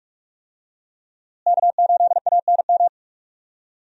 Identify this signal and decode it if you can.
Morse code, heard as a keyed tone, about 44 wpm, 700 Hz. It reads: K8ANM
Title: USA Call Signs - 5 Characters - Repeated 3x - 45wpm